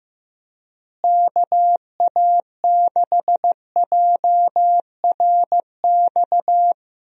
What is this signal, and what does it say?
Morse code, heard as a keyed tone, about 15 wpm, 700 Hz. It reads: KA6JRX
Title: USA Call Signs - 6 Characters - Repeated 3x - 15wpm